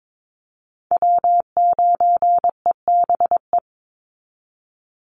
Morse code, heard as a keyed tone, about 22 wpm, 700 Hz. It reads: W9EBE
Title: Call Signs Repeated 3x - 22wpm